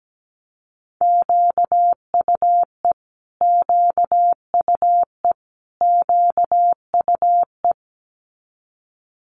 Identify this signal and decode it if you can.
Morse code, heard as a keyed tone, about 17 wpm, 690 Hz. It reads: QUE QUE QUE
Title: Multi-letter Phonemes Repeated 3x 17wpm